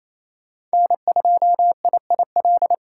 Morse code, heard as a keyed tone, about 27 wpm, 700 Hz. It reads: N2IIL
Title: USA Call Signs - 5 Characters - 8x Character Spacing - 28wpm